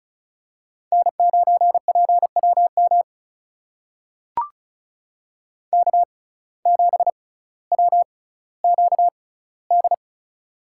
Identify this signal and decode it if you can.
Morse code, heard as a keyed tone, about 33 wpm, spaced as for 18 wpm, 700 Hz. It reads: N9PWM K7WQD
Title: USA Call Signs - 5 Characters - 6x Character Spacing - 35wpm